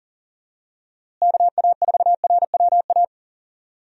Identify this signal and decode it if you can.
Morse code, heard as a keyed tone, about 40 wpm, 700 Hz. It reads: KA4RWA
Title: USA Call Signs - 6 Characters - Repeated 2x - 40wpm